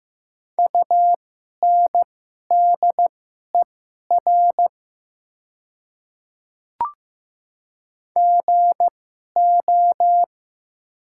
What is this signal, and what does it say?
Morse code, heard as a keyed tone, about 15 wpm, spaced as for 11 wpm, 700 Hz. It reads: UNDER GO
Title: 2x Character Spacing - Top 200 Words - 15wpm